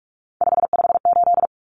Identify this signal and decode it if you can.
Morse code, heard as a keyed tone, about 44 wpm, 700 Hz. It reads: 558
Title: Common Cut Numbers - 3-Digit 45wpm